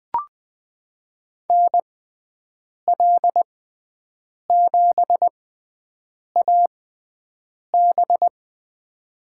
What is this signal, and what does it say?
Morse code, heard as a keyed tone, about 20 wpm, spaced as for 7 wpm, 700 Hz. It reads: NL7AB
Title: USA Call Signs - 6x Character Spacing - 20wpm